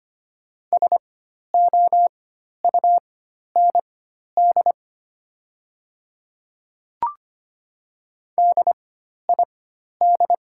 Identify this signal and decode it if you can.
Morse code, heard as a keyed tone, about 25 wpm, spaced as for 12 wpm, 690 Hz. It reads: SOUND DID
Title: 4x Character Spacing - Top 100 Words - 25wpm